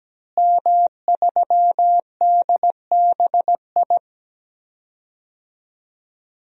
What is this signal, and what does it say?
Morse code, heard as a keyed tone, about 17 wpm, 700 Hz. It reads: M3DBI
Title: DXCC-100 Call Signs - 5 Characters - Repeated 2x - 17wpm